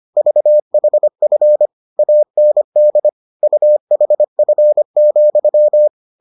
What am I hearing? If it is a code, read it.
Morse code, heard as a keyed tone, about 25 wpm, 600 Hz. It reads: VHF AND UHF,